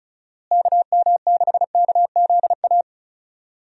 Morse code, heard as a keyed tone, about 36 wpm, 700 Hz. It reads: KM6KZA
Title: USA Call Signs - 6 Characters - Repeated 2x - 35wpm